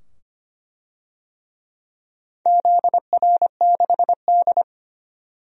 Morse code, heard as a keyed tone, about 25 wpm, 700 Hz. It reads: ZR6D
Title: DXCC-100 Call Signs - 4 Characters - 25wpm